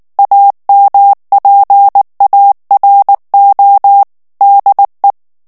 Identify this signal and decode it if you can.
Morse code, heard as a keyed tone, about 19 wpm, 800 Hz. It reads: AMPARO DE
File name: morse mp3 / ESPAGNOL TEXTE EN CLAIR 2 VITESSE 18 MOTS MINUTES